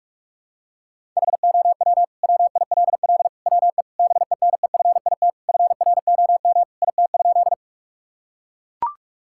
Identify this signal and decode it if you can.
Morse code, heard as a keyed tone, about 41 wpm, 700 Hz. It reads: HOW WILL WE BENEFIT FROM IT?
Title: Sentences from Top 1000 Words - Review Part 6 - 45wpm